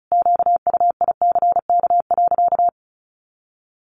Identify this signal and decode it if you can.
Morse code, heard as a keyed tone, about 34 wpm, 700 Hz. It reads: QUICK.
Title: Sentences from Top 500 Words 35wpm